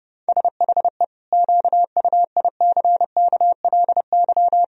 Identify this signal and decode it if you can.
Morse code, heard as a keyed tone, about 32 wpm, 700 Hz. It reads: SHE QUICKLY
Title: Sentences from Top 1000 Words - Review Part 4 - 30wpm